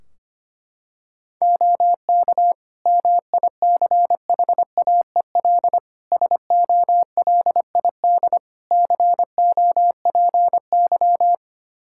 Morse code, heard as a keyed tone, about 25 wpm, 700 Hz. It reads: OK MICHAEL SOLID COPY
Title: K1USN SST QSO Practice 25wpm